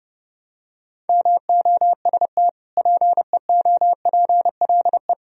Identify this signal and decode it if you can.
Morse code, heard as a keyed tone, about 30 wpm, 700 Hz. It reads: MOST PEOPLE
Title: Sentences from Top 1000 Words - Review Part 1 - 30wpm